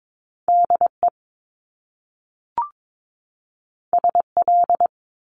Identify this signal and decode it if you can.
Morse code, heard as a keyed tone, about 22 wpm, 700 Hz. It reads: DE SL
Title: Top Two Letter Combinations in Words 22wpm